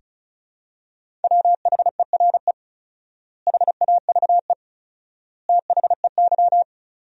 Morse code, heard as a keyed tone, about 34 wpm, 700 Hz. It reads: WHERE HAVE THEY
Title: 4x Word-Spacing - Sentences from Top 200 Words - 35wpm